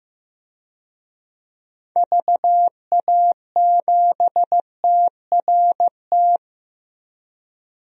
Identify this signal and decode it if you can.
Morse code, heard as a keyed tone, about 16 wpm, 700 Hz. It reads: VA7TRT
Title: DXCC-100 Call Signs - 6 Characters - 15wpm